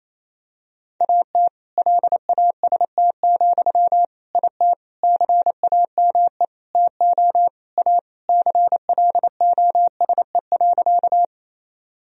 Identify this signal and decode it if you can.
Morse code, heard as a keyed tone, about 28 wpm, 700 Hz. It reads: AT LAST, IT CAME TO A CLOSE.
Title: Sentences from Top 1000 Words - Review Part 1 - 28wpm